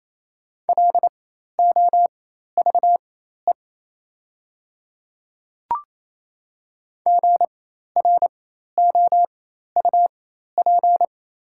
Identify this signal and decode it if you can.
Morse code, heard as a keyed tone, about 27 wpm, spaced as for 13 wpm, 690 Hz. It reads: LOVE GROUP
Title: 4x Character Spacing - Top 500 Words - 28wpm